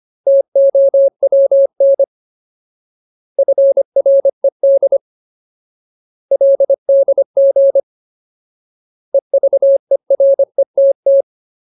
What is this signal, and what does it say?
Morse code, heard as a keyed tone, about 25 wpm, 550 Hz. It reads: TOWN FRED LDG EVERETT
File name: Ultimate Word List / 25WPM Ultimate